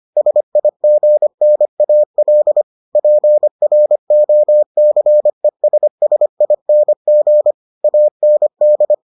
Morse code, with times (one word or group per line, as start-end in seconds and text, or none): 0.2-2.6 SIGNAL
3.0-7.5 PROCESSING
7.8-8.9 AND